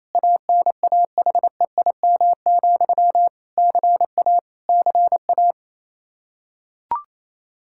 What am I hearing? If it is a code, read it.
Morse code, heard as a keyed tone, about 28 wpm, 700 Hz. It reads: ANAHEIM, CA CA
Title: Practice QSO - Exchange 1 - Beginning - 28wpm